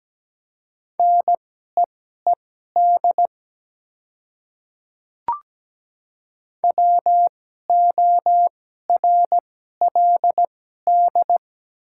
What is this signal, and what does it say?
Morse code, heard as a keyed tone, about 17 wpm, spaced as for 12 wpm, 700 Hz. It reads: NEED WORLD
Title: 2x Character Spacing - Top 300 Words - 17wpm